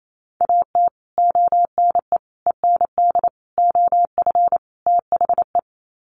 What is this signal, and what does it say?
Morse code, heard as a keyed tone, about 28 wpm, 700 Hz. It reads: AT ONE END OF THE
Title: Sets of 5 Words Encore 28wpm